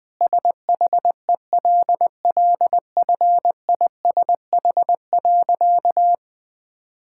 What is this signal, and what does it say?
Morse code, heard as a keyed tone, about 20 wpm, 700 Hz. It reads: SHELLFISH.
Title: All-Age Appropriate Jokes - 20wpm